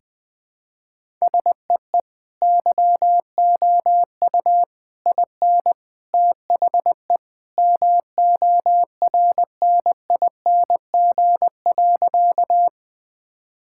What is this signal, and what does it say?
Morse code, heard as a keyed tone, about 20 wpm, 700 Hz. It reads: SEE YOU IN THE MORNING.
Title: Sentences with 3-Word Phrases - 20wpm